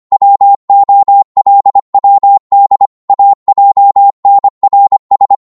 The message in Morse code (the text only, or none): WOLWD AJNRS